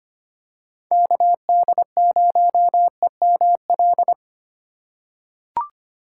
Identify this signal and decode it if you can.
Morse code, heard as a keyed tone, about 25 wpm, 700 Hz. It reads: KD0EML
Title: USA Call Signs - 6 Characters - 4x Character Spacing - 25wpm